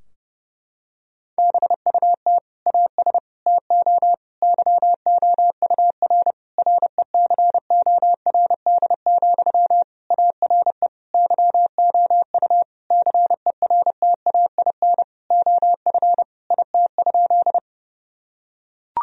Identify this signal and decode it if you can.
Morse code, heard as a keyed tone, about 30 wpm, 700 Hz. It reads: BUT AS TO YOUR RECORD, ARE YOU CERTAIN OF IT?
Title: Sentences from Top 1000 Words - Review Part 3 - 30wpm